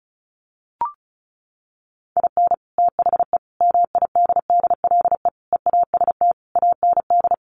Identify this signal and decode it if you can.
Morse code, heard as a keyed tone, about 34 wpm, 700 Hz. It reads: IN THE MIDDLE EAST AND
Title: Sets of 5 Words Encore 35wpm